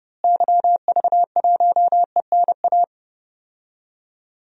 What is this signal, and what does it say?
Morse code, heard as a keyed tone, about 30 wpm, 700 Hz. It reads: YV1ENA
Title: DXCC-100 Call Signs - 6 Characters - Repeated 2x - 30wpm